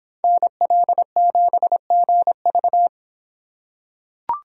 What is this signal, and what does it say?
Morse code, heard as a keyed tone, about 25 wpm, 700 Hz. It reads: NL7GV